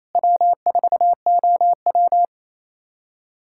Morse code, heard as a keyed tone, about 28 wpm, 700 Hz. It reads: W4OW
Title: USA Callsign DE Callsign - 4 Characters - 28wpm